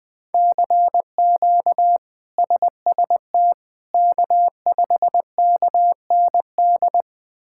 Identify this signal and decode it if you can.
Morse code, heard as a keyed tone, about 20 wpm, 700 Hz. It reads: CQ SST K5KND
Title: K1USN SST QSO Practice - 20wpm